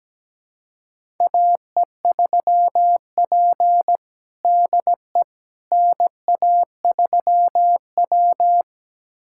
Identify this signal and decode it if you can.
Morse code, heard as a keyed tone, about 17 wpm, 700 Hz. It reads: AE3P DE NA3W
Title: USA Callsign DE Callsign - 4 Characters - 17wpm